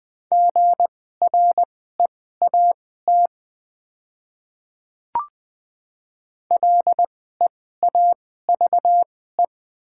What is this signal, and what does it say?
Morse code, heard as a keyed tone, about 20 wpm, spaced as for 14 wpm, 700 Hz. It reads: GREAT LEAVE
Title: 2x Character Spacing - Top 500 Words - 20wpm